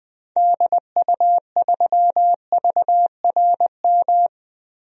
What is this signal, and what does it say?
Morse code, heard as a keyed tone, about 20 wpm, 700 Hz. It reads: DU3VRM
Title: DXCC-100 Call Signs - 6 Characters - 20wpm